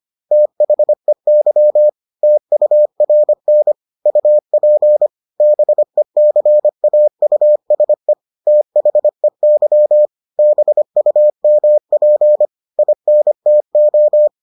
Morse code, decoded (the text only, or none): THEY TURN UP BECAUSE THEY BUMP INTO